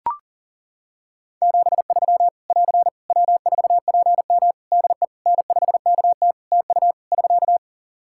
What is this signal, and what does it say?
Morse code, heard as a keyed tone, about 40 wpm, 700 Hz. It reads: 73 <AR> W4PM DE N5KT TU <SK>
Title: K1USN SST QSO Practice 40wpm